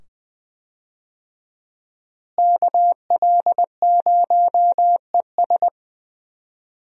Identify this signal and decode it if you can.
Morse code, heard as a keyed tone, about 20 wpm, 700 Hz. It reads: KL0ES
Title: USA Call Signs - 5 Characters - Repeated 2x - 20wpm